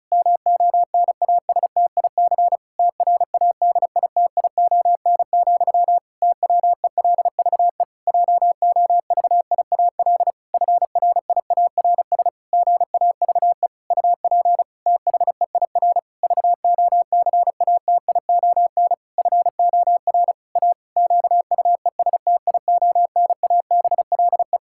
Morse code, heard as a keyed tone, about 34 wpm, 700 Hz. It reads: MONASTIC TRADITION, TWELVE JOVIAL FRIARS GAVE UP THEIR VOCATION FOR A QUESTIONABLE
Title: Pangram Challenge v2 - 35wpm